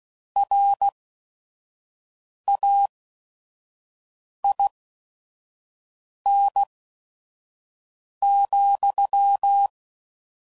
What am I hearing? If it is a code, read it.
Morse code, heard as a keyed tone, about 16 wpm, spaced as for 5 wpm, 800 Hz. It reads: RAIN,
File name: CW Code Groups @5WPM / Weather Sets / Weather 5 5